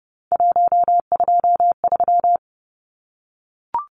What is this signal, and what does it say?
Morse code, heard as a keyed tone, about 30 wpm, 700 Hz. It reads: 123
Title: Numbers - 3-Digit 30wpm